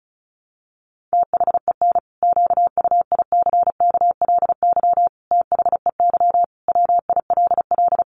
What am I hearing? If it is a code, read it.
Morse code, heard as a keyed tone, about 34 wpm, 700 Hz. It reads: THEN QUICKLY THEY WILL
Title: Sentences from Top 700 Words 35wpm